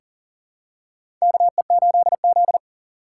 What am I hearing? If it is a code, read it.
Morse code, heard as a keyed tone, about 40 wpm, 700 Hz. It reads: KE8Z
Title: USA Call Signs - 4x Character Spacing - 40wpm